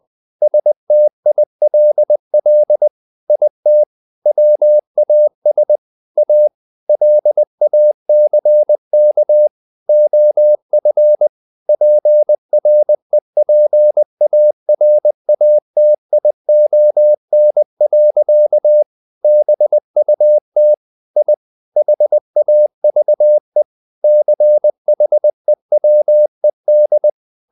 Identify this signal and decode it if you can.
Morse code, heard as a keyed tone, about 20 wpm, 600 Hz. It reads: STILL IT WAS A LACK OF PREPARATION. BUT I HAVE CHEWED
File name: the old man and the sea part 2 20wpm